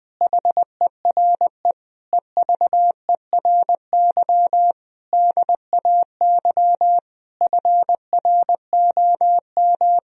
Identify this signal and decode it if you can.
Morse code, heard as a keyed tone, about 20 wpm, 700 Hz. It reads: HERE EVERY DAY FROM
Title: Sentences from Top 1000 Words - Review Part 2 - 20wpm